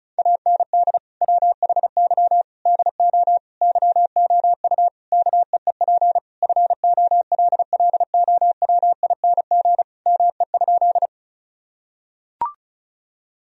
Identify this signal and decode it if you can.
Morse code, heard as a keyed tone, about 36 wpm, 700 Hz. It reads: AND WHY DO YOU KEEP FOLLOWING ME?
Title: Sentences from Top 1000 Words - Review Part 4 - 35wpm